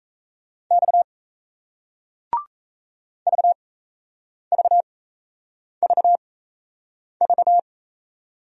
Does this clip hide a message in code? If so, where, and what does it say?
Yes: Morse code from 0.7 to 7.6 s, heard as a keyed tone, about 38 wpm, spaced as for 8 wpm, 700 Hz. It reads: X VVVV